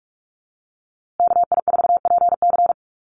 Morse code, heard as a keyed tone, about 44 wpm, 700 Hz. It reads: KI4PC
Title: USA Call Signs Repeated 3x 45wpm